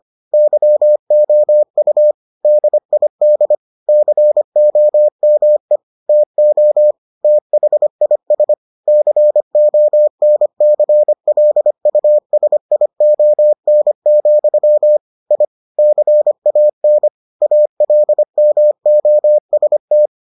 Morse code, heard as a keyed tone, about 25 wpm, 600 Hz. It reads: YOU DID COME TO THIS CONCLUSION, I CAN ALMOST